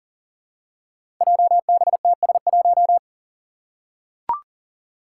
Morse code, heard as a keyed tone, about 40 wpm, 700 Hz. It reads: JBTS1
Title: Five Challenging Letter-Number Combinations 40wpm